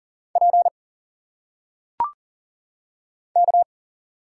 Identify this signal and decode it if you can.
Morse code, heard as a keyed tone, about 40 wpm, 700 Hz. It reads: P K
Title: Single Letters - Phonetic Pronunciation 40wpm